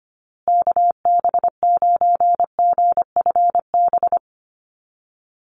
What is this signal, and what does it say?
Morse code, heard as a keyed tone, about 25 wpm, 700 Hz. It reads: KB9GFB
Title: USA Call Signs Repeated 2x 25wpm